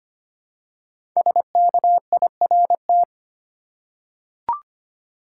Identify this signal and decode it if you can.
Morse code, heard as a keyed tone, about 25 wpm, 700 Hz. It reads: SKIRT
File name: 5 Letter Words - First 3 Letters Given - 25wpm